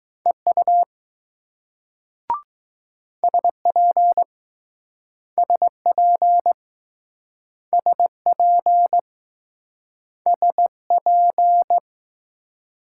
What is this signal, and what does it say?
Morse code, heard as a keyed tone, about 20 wpm, 700 Hz. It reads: EU SP SP SP SP